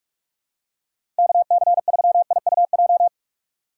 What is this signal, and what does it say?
Morse code, heard as a keyed tone, about 46 wpm, 700 Hz. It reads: KC3IUJ
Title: USA Call Signs - 6 Characters - 3x Character Spacing - 45wpm